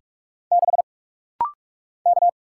Morse code, heard as a keyed tone, about 47 wpm, 700 Hz. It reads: 6 K
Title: Single Letter-Number - Rapid-Fire 45wpm